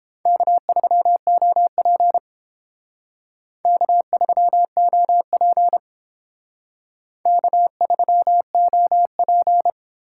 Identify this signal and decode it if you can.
Morse code, heard as a keyed tone, about 28 wpm, 700 Hz. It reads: K3OP K3OP K3OP